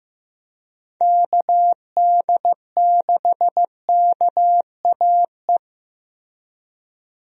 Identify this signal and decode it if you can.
Morse code, heard as a keyed tone, about 15 wpm, 700 Hz. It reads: KD6KAE
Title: USA Call Signs - 6 Characters - Repeated 2x - 15wpm